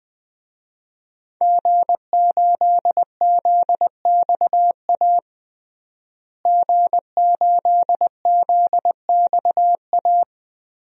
Morse code, heard as a keyed tone, about 21 wpm, 700 Hz. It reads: G8ZXA G8ZXA
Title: DXCC-100 Call Signs - 5 Characters - Repeated 3x - 20wpm